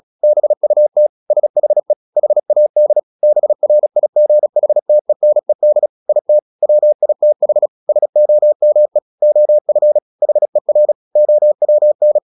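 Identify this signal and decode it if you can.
Morse code, heard as a keyed tone, about 37 wpm, 600 Hz. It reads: BUT SHE HAD BRIGHTENED IT WITH SOME OF HER OWN
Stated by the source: Morse practice at 36 wpm, text of Ken Follett's novel Never, from CW med Gnister.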